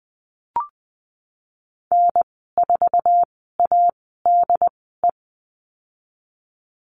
Morse code, heard as a keyed tone, about 19 wpm, spaced as for 14 wpm, 700 Hz. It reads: N4ADE
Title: 088 Lesson 15g - Number 5 - Call Signs 20wpm